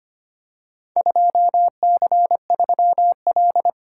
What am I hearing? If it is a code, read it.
Morse code, heard as a keyed tone, about 25 wpm, 700 Hz. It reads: 2C3L
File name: practice / 25wpm / Four Letter-Number Combinations - 25wpm